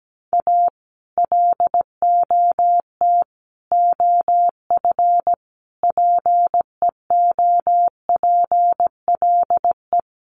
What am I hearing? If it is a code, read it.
Morse code, heard as a keyed tone, about 17 wpm, 700 Hz. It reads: A LOT OF PEOPLE
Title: Sets of 5 Words 17wpm